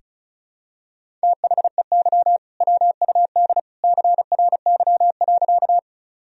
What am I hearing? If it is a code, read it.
Morse code, heard as a keyed tone, about 34 wpm, 700 Hz. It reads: THEY WUD CRY.
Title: Sentences from Top 500 Words and Abbreviations 35wpm